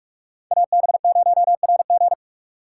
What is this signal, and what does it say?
Morse code, heard as a keyed tone, about 46 wpm, 700 Hz. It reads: AB0RG
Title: USA Call Signs - 5 Characters - 4x Character Spacing - 45wpm